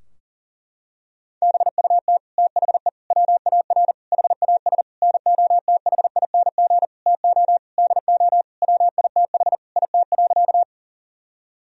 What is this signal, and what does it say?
Morse code, heard as a keyed tone, about 40 wpm, 700 Hz. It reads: BUT THE WAR HAS NOTHING TO DO WITH IT.
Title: Sentences from Top 1000 Words - Review Part 2 - 40wpm